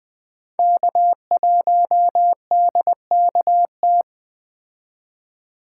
Morse code, heard as a keyed tone, about 19 wpm, 700 Hz. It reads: K1DKT